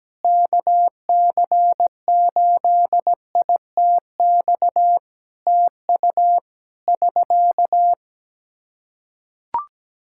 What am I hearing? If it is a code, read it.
Morse code, heard as a keyed tone, about 17 wpm, 700 Hz. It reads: KC8ITX TU <SK>
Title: K1USN SST QSO Practice 17wpm